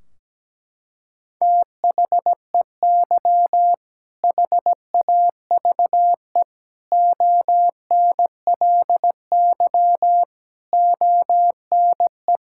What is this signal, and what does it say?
Morse code, heard as a keyed tone, about 17 wpm, 700 Hz. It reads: THEY HAVE ONLY ONE
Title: Sentences from Top 1600 Words 17wpm